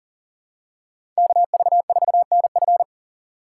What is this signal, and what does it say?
Morse code, heard as a keyed tone, about 40 wpm, 700 Hz. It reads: KV4NF
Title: USA Call Signs - 5 Characters - 4x Character Spacing - 40wpm